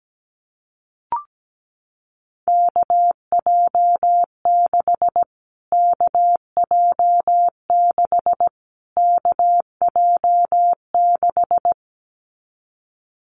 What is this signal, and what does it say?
Morse code, heard as a keyed tone, about 17 wpm, 700 Hz. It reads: KJ6 KJ6 KJ6
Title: Callsign Prefixes for Top 25 DXCC - Repeated 3x - 17wpm